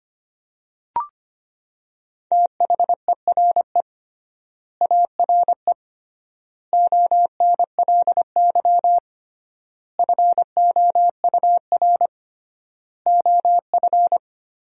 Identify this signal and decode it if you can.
Morse code, heard as a keyed tone, about 25 wpm, 700 Hz. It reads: THERE ARE ONLY FOUR OF
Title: 3x Word-Spacing - Sentences from Top 300 Words - 25wpm